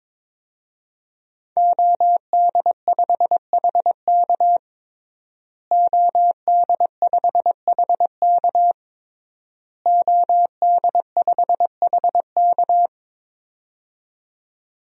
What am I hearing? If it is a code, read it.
Morse code, heard as a keyed tone, about 22 wpm, 700 Hz. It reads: OD5HK OD5HK OD5HK
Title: DXCC-100 Call Signs - 5 Characters - Repeated 3x - 22wpm